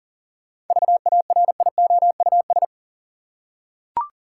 Morse code, heard as a keyed tone, about 40 wpm, 700 Hz. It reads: VARIOUS
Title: Seven Letter Words - 40wpm